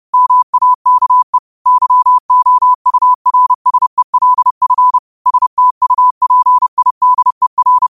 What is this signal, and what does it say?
Morse code, heard as a keyed tone, about 30 wpm, 1000 Hz. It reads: MAKE YOURSELF STUPIDER